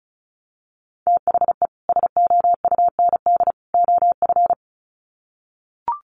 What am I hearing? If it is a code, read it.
Morse code, heard as a keyed tone, about 34 wpm, 700 Hz. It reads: THE SOUND OF
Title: Sets of 3 Words 35wpm